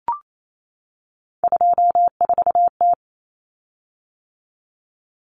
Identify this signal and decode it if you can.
Morse code, heard as a keyed tone, about 28 wpm, 700 Hz. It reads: 24T
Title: Common Cut Numbers - 3-Digit 28wpm